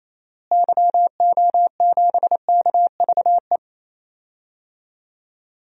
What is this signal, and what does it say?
Morse code, heard as a keyed tone, about 28 wpm, 700 Hz. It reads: YO7KVE
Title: DXCC-100 Call Signs - 6 Characters - Repeated 3x - 28wpm